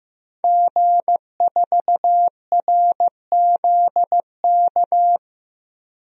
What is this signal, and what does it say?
Morse code, heard as a keyed tone, about 15 wpm, 700 Hz. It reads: G4RZK
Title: DXCC-100 Call Signs - 5 Characters - Repeated 3x - 15wpm